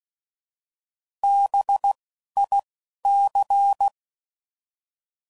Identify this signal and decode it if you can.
Morse code, heard as a keyed tone, about 16 wpm, 800 Hz. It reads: B I C